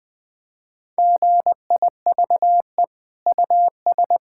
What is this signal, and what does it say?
Morse code, heard as a keyed tone, about 20 wpm, 700 Hz. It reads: GIVE US
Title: Sentences from Top 1000 Words - Review Part 5 - 20wpm